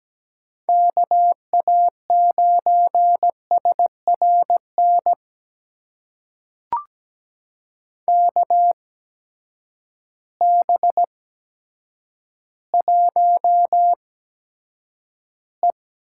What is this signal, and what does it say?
Morse code, heard as a keyed tone, about 17 wpm, spaced as for 8 wpm, 700 Hz. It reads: KA9SRN KB1E
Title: USA Call Signs - 6 Characters - 8x Character Spacing - 17wpm